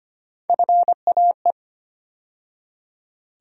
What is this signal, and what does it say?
Morse code, heard as a keyed tone, about 25 wpm, 700 Hz. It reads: FAE
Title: USA Call Sign Postfixes - 3 Letters 25wpm